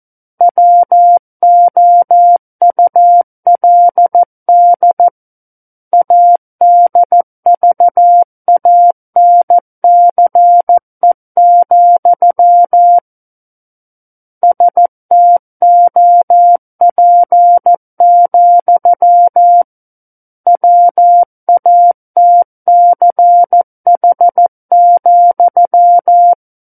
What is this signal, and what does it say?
Morse code, heard as a keyed tone, about 14 wpm, 700 Hz. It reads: WOULD ADVANCE, STOP, WATCH,
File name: Book One / 05-Chapter 5 - 14 WPM